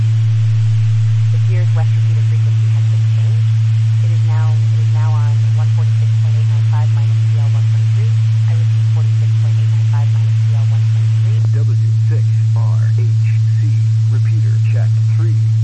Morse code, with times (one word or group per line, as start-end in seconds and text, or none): none